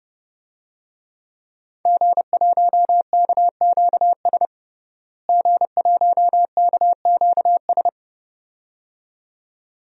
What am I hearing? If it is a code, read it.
Morse code, heard as a keyed tone, about 30 wpm, 700 Hz. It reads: G1KQS G1KQS
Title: DXCC-100 Call Signs - 5 Characters - Repeated 2x - 30wpm